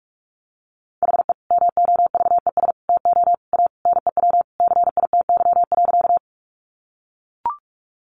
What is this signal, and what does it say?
Morse code, heard as a keyed tone, about 44 wpm, 700 Hz. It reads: HE MOVES TO A NEW CITY.
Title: Easy Phrases 45wpm